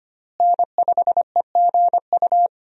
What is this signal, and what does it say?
Morse code, heard as a keyed tone, about 25 wpm, 700 Hz. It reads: N5EGU
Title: USA Call Signs - 5 Characters - Repeated 3x - 25wpm